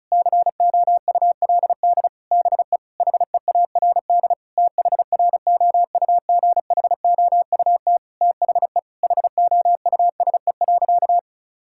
Morse code, heard as a keyed tone, about 36 wpm, 700 Hz. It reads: COULD BE HEARD THROUGHOUT THE HOUSE.
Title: Sentences from Top 1000 Words - Review Part 5 - 35wpm